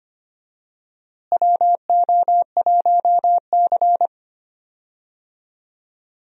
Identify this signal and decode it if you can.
Morse code, heard as a keyed tone, about 25 wpm, 700 Hz. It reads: WO1C
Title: USA Call Signs - 4 Characters - Repeated 3x - 25wpm